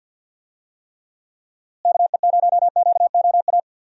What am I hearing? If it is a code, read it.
Morse code, heard as a keyed tone, about 51 wpm, 700 Hz. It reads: KE0QOA
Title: USA Call Signs - 6 Characters - Repeated 3x - 50wpm